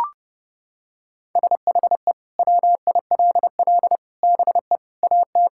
Morse code, heard as a keyed tone, about 32 wpm, 700 Hz. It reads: SHE WILL BE AT
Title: Sentences from Top 1600 Words 30wpm